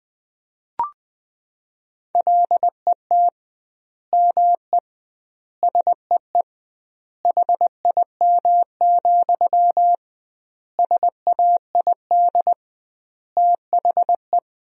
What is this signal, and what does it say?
Morse code, heard as a keyed tone, about 20 wpm, 700 Hz. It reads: LET ME SEE HIM, SAID THE
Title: 2x Word-Spacing - Sentences from Top 400 Words - 20wpm